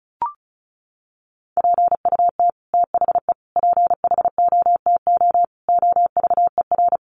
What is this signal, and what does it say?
Morse code, heard as a keyed tone, about 34 wpm, 700 Hz. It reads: PUT THE PHOTO OVER
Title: Sentences from Top 1200 Words 35wpm